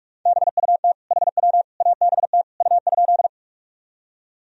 Morse code, heard as a keyed tone, about 41 wpm, 700 Hz. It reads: BUT HW ABT U?
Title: Sentences from Top 100 Words and Abbreviations 45wpm